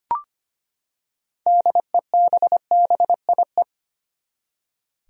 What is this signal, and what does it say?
Morse code, heard as a keyed tone, about 25 wpm, 700 Hz. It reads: DEBBIE
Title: USA Common Names - 25wpm